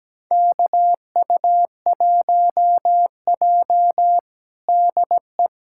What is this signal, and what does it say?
Morse code, heard as a keyed tone, about 17 wpm, 700 Hz. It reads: KU1J DE
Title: USA Callsign DE Callsign - 4 Characters - 17wpm